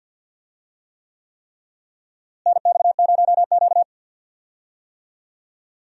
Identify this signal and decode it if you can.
Morse code, heard as a keyed tone, about 52 wpm, 700 Hz. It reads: NX0Q
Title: USA Call Signs - 4 Characters - Repeated 3x - 50wpm